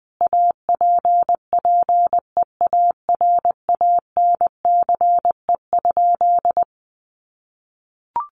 Morse code, heard as a keyed tone, about 19 wpm, 700 Hz. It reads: APPEARANCE?
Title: Sentences from Top 1100 Words 20wpm